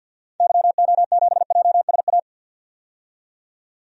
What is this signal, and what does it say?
Morse code, heard as a keyed tone, about 52 wpm, 700 Hz. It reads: YO7JSA
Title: DXCC-100 Call Signs - 6 Characters - 50wpm